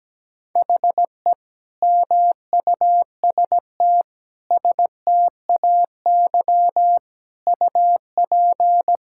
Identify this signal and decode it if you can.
Morse code, heard as a keyed tone, about 17 wpm, 700 Hz. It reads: HE MUST STAY UP
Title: Sentences from Top 1600 Words 17wpm